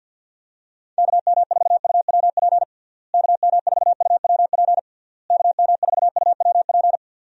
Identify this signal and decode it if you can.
Morse code, heard as a keyed tone, about 49 wpm, 700 Hz. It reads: KM4UWP KM4UWP KM4UWP
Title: USA Call Signs - 6 Characters - Repeated 3x - 50wpm